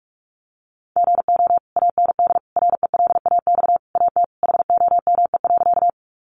Morse code, heard as a keyed tone, about 44 wpm, 700 Hz. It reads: GO AND RELAX AT HOME.
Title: Sentences from Top 1400 Words 45wpm